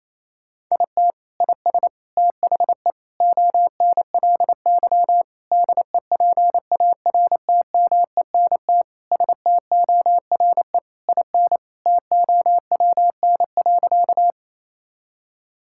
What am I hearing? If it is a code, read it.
Morse code, heard as a keyed tone, about 28 wpm, 700 Hz. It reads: IT IS THE ONLY DEPARTMENT STORE IN TOWN.
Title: Sentences from Top 1000 Words - Review Part 5 - 28wpm